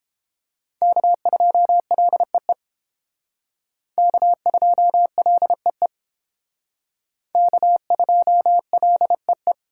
Morse code, heard as a keyed tone, about 32 wpm, 700 Hz. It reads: K2LEE K2LEE K2LEE